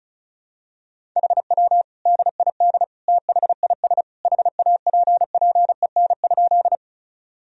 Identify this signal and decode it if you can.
Morse code, heard as a keyed tone, about 36 wpm, 700 Hz. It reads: HW DID THIS HAPPEN?
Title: Sentences from Top 400 Words and Abbreviations 35wpm